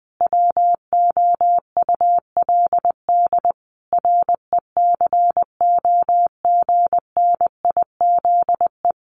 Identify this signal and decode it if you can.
Morse code, heard as a keyed tone, about 20 wpm, 700 Hz. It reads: WOULD RECOGNIZE
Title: Sentences from Top 1400 Words 20wpm